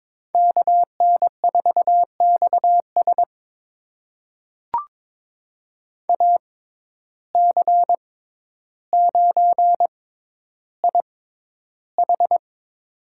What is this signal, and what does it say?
Morse code, heard as a keyed tone, about 22 wpm, spaced as for 11 wpm, 700 Hz. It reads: KN4XS AC9IH
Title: USA Call Signs - 5 Characters - 6x Character Spacing - 22wpm